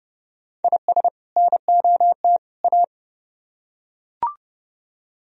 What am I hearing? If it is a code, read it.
Morse code, heard as a keyed tone, about 28 wpm, 700 Hz. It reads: IS NOT A